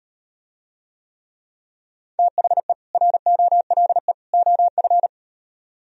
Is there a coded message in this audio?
Yes: Morse code from 2.2 to 5.1 s, heard as a keyed tone, about 38 wpm, 700 Hz. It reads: THE ROLE OF